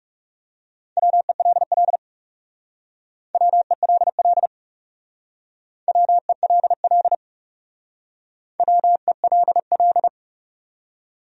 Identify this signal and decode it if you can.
Morse code, heard as a keyed tone, about 38 wpm, 700 Hz. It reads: WELL WELL WELL WELL